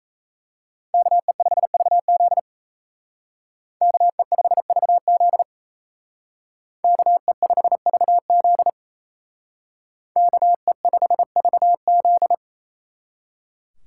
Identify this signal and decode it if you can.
Morse code, heard as a keyed tone, about 36 wpm, 700 Hz. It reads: KE5VZ KE5VZ KE5VZ KE5VZ